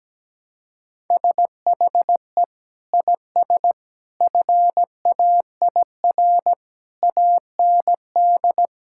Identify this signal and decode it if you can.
Morse code, heard as a keyed tone, about 17 wpm, 700 Hz. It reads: SHE IS FAIR AND
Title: Sentences from Top 1600 Words 17wpm